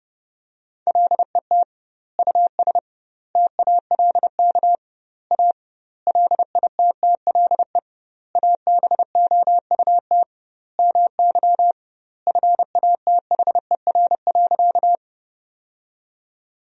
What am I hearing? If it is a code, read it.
Morse code, heard as a keyed tone, about 32 wpm, 700 Hz. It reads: LET US TALK A LITTLE ABOUT MY FATHER.
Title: 2x Word-Spacing - Sentences from Top 400 Words - 30wpm